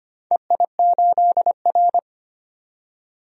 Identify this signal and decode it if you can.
Morse code, heard as a keyed tone, about 25 wpm, 700 Hz. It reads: EI8R
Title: DXCC-100 Call Signs - 4 Characters - 25wpm